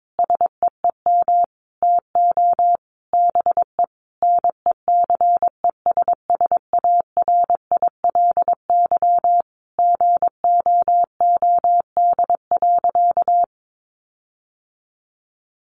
Morse code, heard as a keyed tone, about 22 wpm, 700 Hz. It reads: SEEM TO BE NECESSARILY GOOD.
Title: Sentences from Top 1400 Words 22wpm